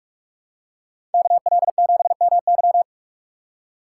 Morse code, heard as a keyed tone, about 46 wpm, 700 Hz. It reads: KL7MY
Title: USA Call Signs - 6x Character Spacing - 45wpm